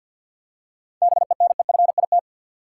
Morse code, heard as a keyed tone, about 47 wpm, 700 Hz. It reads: BENEFIT
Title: Seven Letter Words - 50wpm